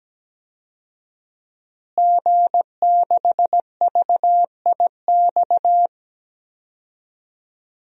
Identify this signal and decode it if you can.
Morse code, heard as a keyed tone, about 17 wpm, 700 Hz. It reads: G6VIX
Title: DXCC-100 Call Signs - 5 Characters - Repeated 2x - 17wpm